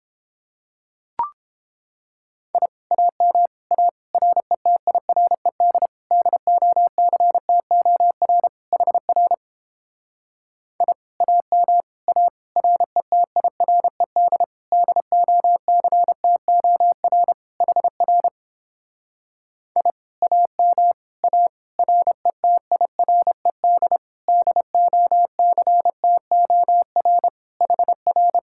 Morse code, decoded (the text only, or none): I AM A RETIRED DOCTOR HR I AM A RETIRED DOCTOR HR I AM A RETIRED DOCTOR HR